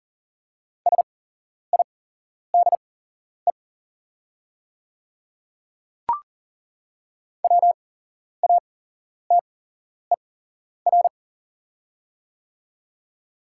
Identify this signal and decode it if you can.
Morse code, heard as a keyed tone, about 40 wpm, spaced as for 11 wpm, 690 Hz. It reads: SIDE WATER
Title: 8x Character Spacing - Top 200 Words - 40wpm